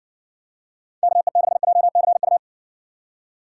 Morse code, heard as a keyed tone, about 60 wpm, 700 Hz. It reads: KE6JCU